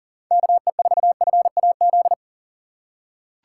Morse code, heard as a keyed tone, about 40 wpm, 700 Hz. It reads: KE4FAZ
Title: USA Call Signs - 6 Characters - 4x Character Spacing - 40wpm